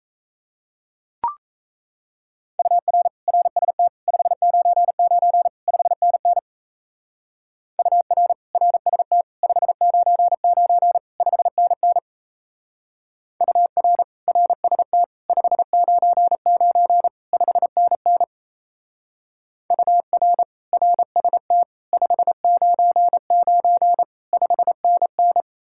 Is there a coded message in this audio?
Yes: Morse code from 2.6 to 25.4 s, heard as a keyed tone, about 35 wpm, 700 Hz. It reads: UR RST 599 5NN UR RST 599 5NN UR RST 599 5NN UR RST 599 5NN